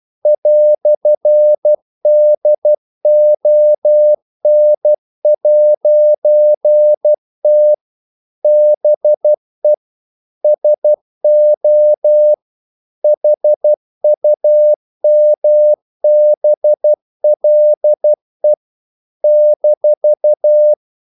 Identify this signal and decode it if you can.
Morse code, heard as a keyed tone, about 12 wpm, 600 Hz. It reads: "DON'T BE SO HUMBLE -
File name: CW / Practice / EnglishQuotations0112